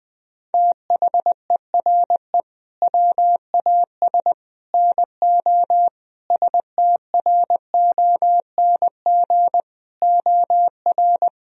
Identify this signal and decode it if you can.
Morse code, heard as a keyed tone, about 21 wpm, 700 Hz. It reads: THERE WAS NO STRONG OR
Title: Sentences from Top 1000 Words - Review Part 4 - 20wpm